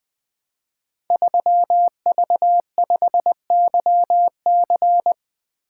Morse code, heard as a keyed tone, about 20 wpm, 700 Hz. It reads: 3V5YC
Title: Code Groups - 5 Letter:Numbers - 20wpm